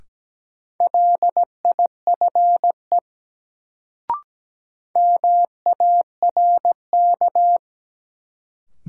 Morse code, heard as a keyed tone, about 17 wpm, 700 Hz. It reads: LIFE MARK
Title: Sentences from Top 300 Words - New Word Drill - 17wpm